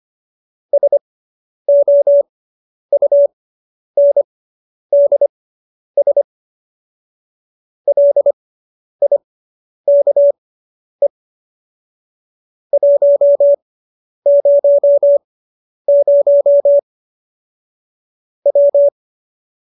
Morse code, heard as a keyed tone, about 25 wpm, spaced as for 10 wpm, 570 Hz. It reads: SOUNDS LIKE 100 W
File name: QSO220 10